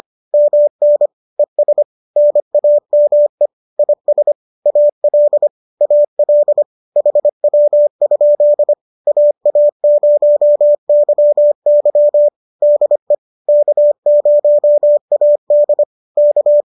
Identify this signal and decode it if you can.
Morse code, heard as a keyed tone, about 25 wpm, 590 Hz. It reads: MN ES NAME IS AL AL HW? AA0YY DE K0AD K
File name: qso213 25